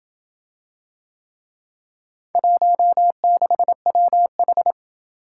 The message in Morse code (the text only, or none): 16WH